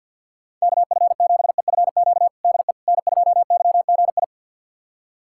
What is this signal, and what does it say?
Morse code, heard as a keyed tone, about 46 wpm, 700 Hz. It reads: KF7EFQ DE N2YGI
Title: USA Callsign DE Callsign 50wpm